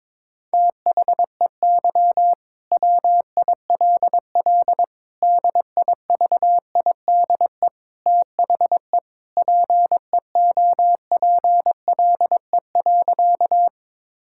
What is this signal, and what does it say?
Morse code, heard as a keyed tone, about 22 wpm, 700 Hz. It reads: THEY WILL DIVIDE THE PEOPLE.
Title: Sentences from Top 1600 Words 22wpm